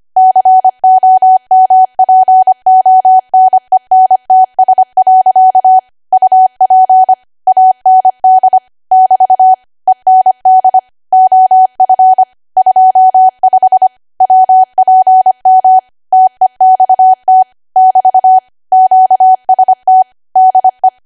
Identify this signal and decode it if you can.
Morse code, heard as a keyed tone, about 25 wpm, 750 Hz. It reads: COMPONENTS. UP AND <BT> END OF 25 WPM TEXT <BT> QST DE
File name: cwa / W1AW-Level3 / 25WPM